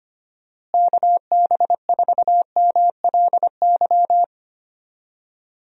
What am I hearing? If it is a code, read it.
Morse code, heard as a keyed tone, about 25 wpm, 700 Hz. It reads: KB4MLY
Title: USA Call Signs - 6 Characters - Repeated 2x - 25wpm